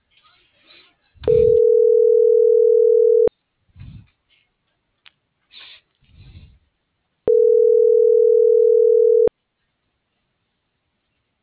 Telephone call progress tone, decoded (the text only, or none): ringback tone